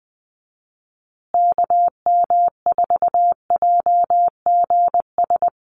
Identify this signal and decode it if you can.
Morse code, heard as a keyed tone, about 20 wpm, 700 Hz. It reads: KM4JGS
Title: USA Call Signs Repeated 3x 20wpm